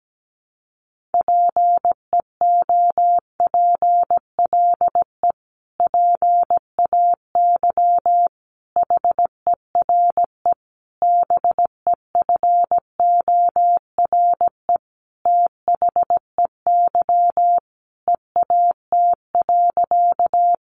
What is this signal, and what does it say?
Morse code, heard as a keyed tone, about 17 wpm, 700 Hz. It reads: PEOPLE PAY HERE BEFORE THEY EAT.
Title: Sentences from Top 700 Words 17wpm